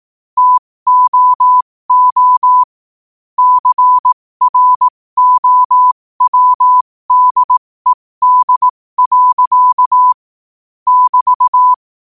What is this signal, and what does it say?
Morse code, heard as a keyed tone, about 18 wpm, spaced as for 15 wpm, 1000 Hz. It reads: TOO CROWDED. <BT>
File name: qotdfiles / 15wpm